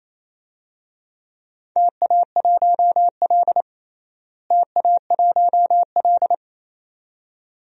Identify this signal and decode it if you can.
Morse code, heard as a keyed tone, about 28 wpm, 700 Hz. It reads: TA1L TA1L
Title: DXCC-100 Call Signs - 4 Characters - Repeated 2x - 28wpm